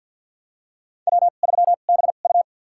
Morse code, heard as a keyed tone, about 52 wpm, spaced as for 37 wpm, 700 Hz. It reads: W3BU
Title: USA Call Signs - 4 Characters - 2x Character Spacing - 50wpm